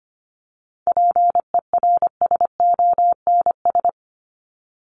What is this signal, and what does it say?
Morse code, heard as a keyed tone, about 25 wpm, 700 Hz. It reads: PERSONS
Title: ES:S Suffix - Top 100 Words with Base Words 25wpm